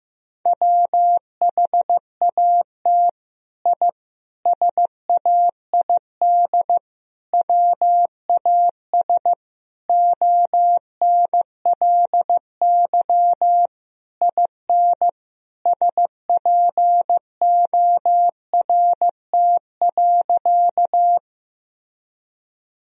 Morse code, heard as a keyed tone, about 16 wpm, 700 Hz. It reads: WHAT I SAID WAS ONLY IN SPORT.
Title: Sentences from Top 1000 Words - Review Part 4 - 15wpm